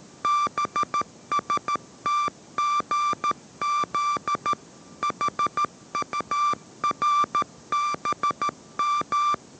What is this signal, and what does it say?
Morse code, heard as a keyed tone, about 13 wpm, 1180 Hz. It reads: BSTGZ HURBM